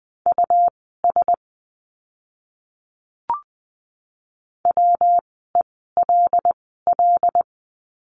Morse code, heard as a keyed tone, about 20 wpm, spaced as for 15 wpm, 700 Hz. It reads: US WELL
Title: 115 Lesson 19f - Letter w - Words 20wpm